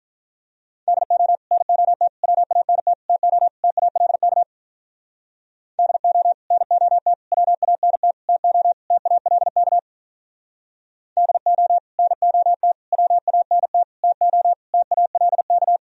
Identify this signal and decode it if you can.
Morse code, heard as a keyed tone, about 46 wpm, 700 Hz. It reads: DO NOT WANT TO TALK DO NOT WANT TO TALK DO NOT WANT TO TALK